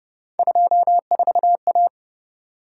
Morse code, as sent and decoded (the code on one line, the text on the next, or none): ..--- ....- .-
24A